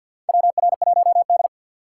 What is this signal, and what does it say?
Morse code, heard as a keyed tone, about 50 wpm, 700 Hz. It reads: WR1D
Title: USA Call Signs - 4 Characters - 3x Character Spacing - 50wpm